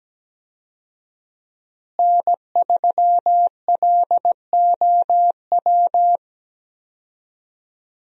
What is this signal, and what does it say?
Morse code, heard as a keyed tone, about 17 wpm, 700 Hz. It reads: N3LOW
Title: USA Call Signs - 5 Characters - Repeated 3x - 17wpm